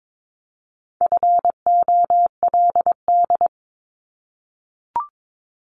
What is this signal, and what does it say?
Morse code, heard as a keyed tone, about 22 wpm, 700 Hz. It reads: FOLD
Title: Multi-letter Phonemes 22wpm